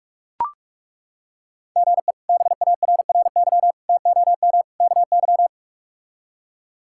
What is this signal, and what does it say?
Morse code, heard as a keyed tone, about 43 wpm, 700 Hz. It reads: GE BARRY TOM KY
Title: K1USN SST QSO Practice - 45wpm